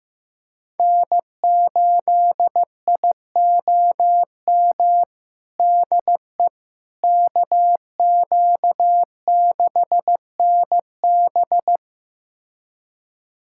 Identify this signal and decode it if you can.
Morse code, heard as a keyed tone, about 15 wpm, 700 Hz. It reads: N8IOM DE KQ6NB
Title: USA Callsign DE Callsign - 5 Characters - 15wpm